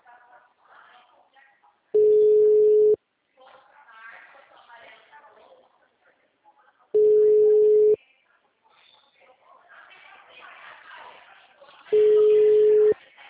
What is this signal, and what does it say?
Ringback tone (Europe, 425 Hz), with speech.